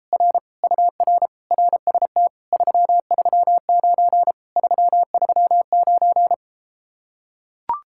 Morse code, heard as a keyed tone, about 33 wpm, 700 Hz. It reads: R UR RST 339 339